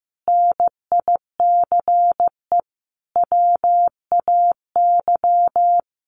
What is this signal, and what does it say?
Morse code, heard as a keyed tone, about 15 wpm, 700 Hz. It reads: NICE WAY
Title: Sentences from Top 1100 Words 15wpm